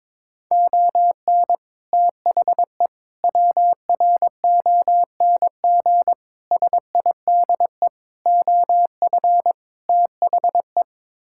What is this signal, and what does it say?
Morse code, heard as a keyed tone, about 23 wpm, 700 Hz. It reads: ON THE WRONG SIDE OF THE
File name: practice / 22wpm / Sets of 6 Words - 22wpm